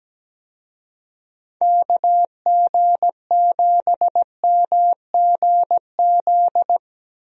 Morse code, heard as a keyed tone, about 17 wpm, 700 Hz. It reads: KG7MGZ
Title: USA Call Signs - 6 Characters - 17wpm